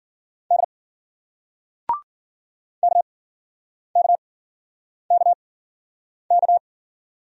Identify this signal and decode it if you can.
Morse code, heard as a keyed tone, about 53 wpm, spaced as for 9 wpm, 700 Hz. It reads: D KKKK